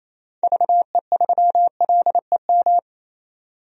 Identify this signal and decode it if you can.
Morse code, heard as a keyed tone, about 28 wpm, 700 Hz. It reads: VE3LEM
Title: DXCC-100 Call Signs - 6 Characters - 28wpm